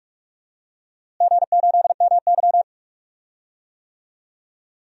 Morse code, heard as a keyed tone, about 41 wpm, 690 Hz. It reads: G8MY
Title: DXCC-100 Call Signs - 4 Characters - Repeated 3x - 45wpm